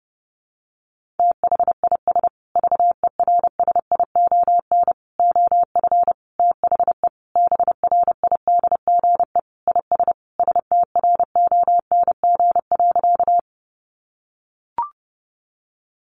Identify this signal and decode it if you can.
Morse code, heard as a keyed tone, about 30 wpm, 700 Hz. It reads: THIS VERSION OF THE BRIDGE IS STRONG.
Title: Sentences from Top 1400 Words 30wpm